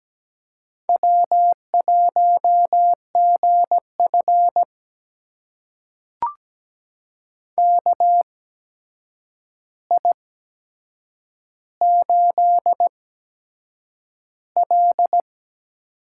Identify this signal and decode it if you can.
Morse code, heard as a keyed tone, about 17 wpm, spaced as for 7 wpm, 700 Hz. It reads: W1GF KI8L
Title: USA Call Signs - 4 Characters - 8x Character Spacing - 17wpm